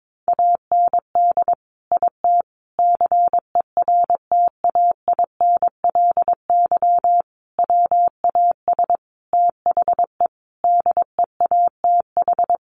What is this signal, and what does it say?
Morse code, heard as a keyed tone, about 22 wpm, 700 Hz. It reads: AND IT CERTAINLY WAS THE DEATH
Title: Sentences from Top 1100 Words 22wpm